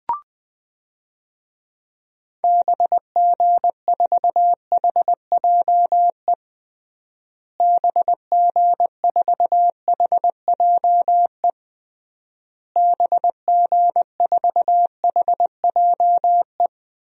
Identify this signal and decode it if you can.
Morse code, heard as a keyed tone, about 20 wpm, 700 Hz. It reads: BG4HJE BG4HJE BG4HJE
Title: DXCC-100 Call Signs - 6 Characters - Repeated 3x - 20wpm